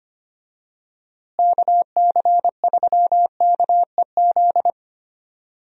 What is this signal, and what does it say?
Morse code, heard as a keyed tone, about 25 wpm, 700 Hz. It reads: KC3KEZ
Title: USA Call Signs - 6 Characters - Repeated 3x - 25wpm